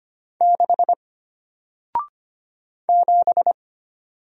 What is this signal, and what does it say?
Morse code, heard as a keyed tone, about 25 wpm, 700 Hz. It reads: 6 7
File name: Single-Letter-Number-25wpm